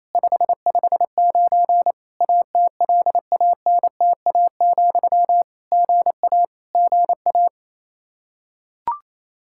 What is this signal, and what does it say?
Morse code, heard as a keyed tone, about 28 wpm, 700 Hz. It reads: 559 ATLANTA, GA GA
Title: K1USN SST QSO Practice 28wpm